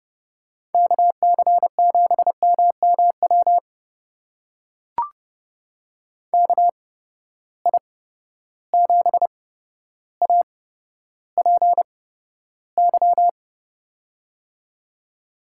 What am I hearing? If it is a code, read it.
Morse code, heard as a keyed tone, about 30 wpm, spaced as for 13 wpm, 700 Hz. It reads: KC7MMW KI7APY
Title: USA Call Signs - 6 Characters - 8x Character Spacing - 30wpm